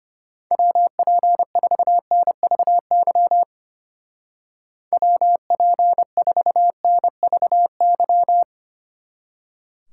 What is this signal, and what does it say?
Morse code, heard as a keyed tone, about 31 wpm, 700 Hz. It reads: WP4NVY WP4NVY